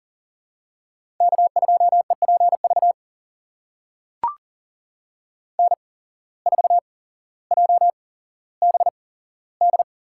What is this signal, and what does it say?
Morse code, heard as a keyed tone, about 42 wpm, spaced as for 17 wpm, 700 Hz. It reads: K2EPV N4JBD
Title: USA Call Signs - 5 Characters - 8x Character Spacing - 40wpm